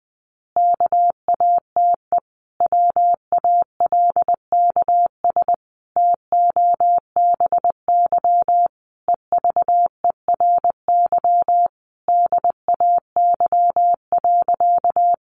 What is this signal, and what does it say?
Morse code, heard as a keyed tone, about 20 wpm, 700 Hz. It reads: KATE WALKS TOBY EVERY DAY.
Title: Easy Phrases 20wpm